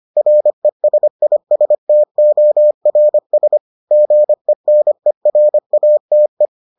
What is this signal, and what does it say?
Morse code, heard as a keyed tone, about 25 wpm, 600 Hz. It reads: RESISTORS GENERATE